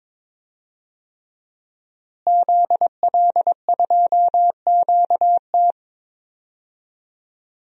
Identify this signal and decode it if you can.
Morse code, heard as a keyed tone, about 23 wpm, 700 Hz. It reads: ZL2QT
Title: DXCC-100 Call Signs - 5 Characters - Repeated 3x - 22wpm